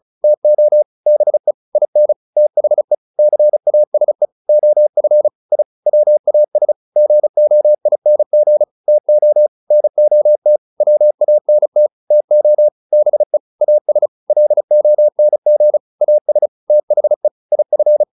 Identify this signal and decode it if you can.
Morse code, heard as a keyed tone, about 34 wpm, 600 Hz. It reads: TO BE IN THE CASE OF I WAS GOING TO NOT WANT TO BE AS LONG AS THE IF